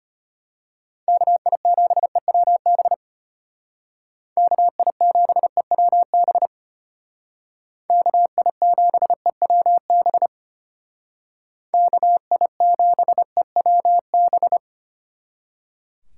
Morse code, heard as a keyed tone, about 33 wpm, 700 Hz. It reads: KI7EWB KI7EWB KI7EWB KI7EWB